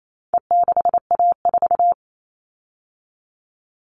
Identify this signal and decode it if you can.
Morse code, heard as a keyed tone, about 28 wpm, 700 Hz. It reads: E6A4
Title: All Cut Numbers - 4-Digit 28wpm